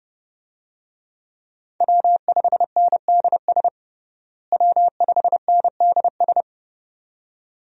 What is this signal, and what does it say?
Morse code, heard as a keyed tone, about 32 wpm, 700 Hz. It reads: W5NDS W5NDS
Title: USA Call Signs - 5 Characters - Repeated 2x - 30wpm